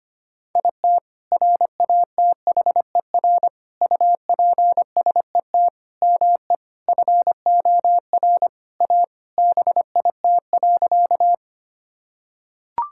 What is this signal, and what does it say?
Morse code, heard as a keyed tone, about 25 wpm, 700 Hz. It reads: IT RATHER UPSET ME FOR A BIT.
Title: Sentences from Top 1000 Words - Review Part 5 - 25wpm